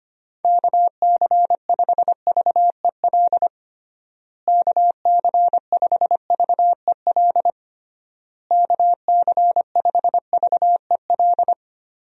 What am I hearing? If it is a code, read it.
Morse code, heard as a keyed tone, about 25 wpm, 700 Hz. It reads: KC5VEL KC5VEL KC5VEL
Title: USA Call Signs - 6 Characters - Repeated 3x - 25wpm